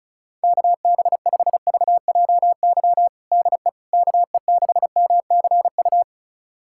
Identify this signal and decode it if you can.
Morse code, heard as a keyed tone, about 34 wpm, 700 Hz. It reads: KB5VJY DE KE6MCU
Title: USA Callsign DE Callsign 35wpm